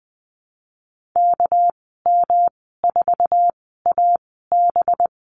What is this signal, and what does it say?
Morse code, heard as a keyed tone, about 20 wpm, spaced as for 15 wpm, 700 Hz. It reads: KM4AB
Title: 207 Lesson 33g - Letter j - Call Signs 20wpm